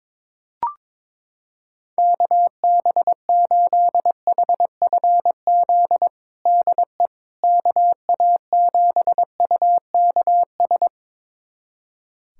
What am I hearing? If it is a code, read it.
Morse code, heard as a keyed tone, about 22 wpm, 700 Hz. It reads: KB8HFZ DE KA7UKS
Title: USA Callsign DE Callsign - 6 Characters - 22wpm